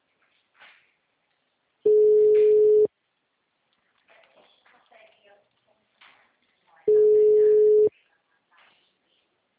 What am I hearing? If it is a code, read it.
Ringback tone (Europe, 425 Hz), with speech.